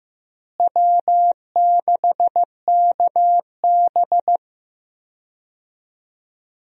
Morse code, heard as a keyed tone, about 16 wpm, 700 Hz. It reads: W6KB
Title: USA Call Signs - 4 Characters - Repeated 3x - 15wpm